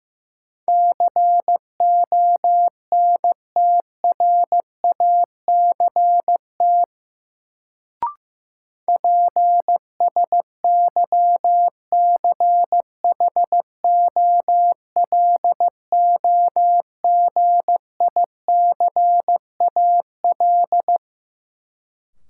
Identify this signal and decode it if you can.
Morse code, heard as a keyed tone, about 15 wpm, 700 Hz. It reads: CONTRACT PSYCHOLOGICAL
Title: Sentences from Top 1400 Words - New Word Drill Alt - 15wpm